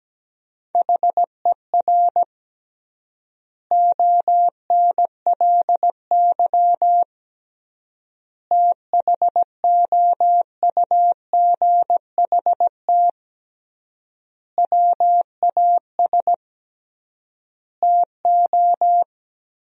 Morse code, heard as a keyed tone, about 17 wpm, 700 Hz. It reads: HER ONLY THOUGHT WAS TO
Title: 3x Word-Spacing - Sentences from Top 400 Words - 17wpm